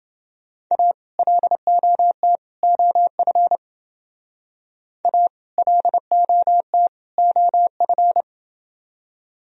Morse code, heard as a keyed tone, about 29 wpm, 700 Hz. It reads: A LOT OF A LOT OF